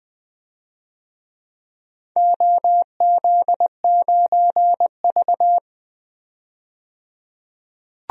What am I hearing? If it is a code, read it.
Morse code, heard as a keyed tone, about 20 wpm, 700 Hz. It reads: OZ9V
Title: DXCC-100 Call Signs - 4 Characters - 20wpm